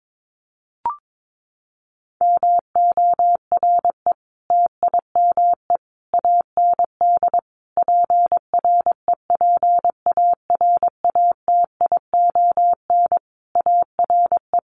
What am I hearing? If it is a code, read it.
Morse code, heard as a keyed tone, about 22 wpm, 700 Hz. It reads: MORE TIME AND PREPARATION ARE
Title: Sentences from Top 1400 Words 22wpm